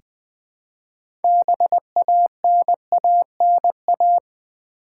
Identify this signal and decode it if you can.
Morse code, heard as a keyed tone, about 20 wpm, 700 Hz. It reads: BANANA
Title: Build A Word - 2 Letters At A Time - 6-Letter Words - 20wpm